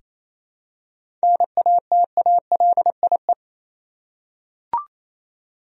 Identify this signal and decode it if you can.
Morse code, heard as a keyed tone, about 28 wpm, 700 Hz. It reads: NATALIE
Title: USA Common Names - 28wpm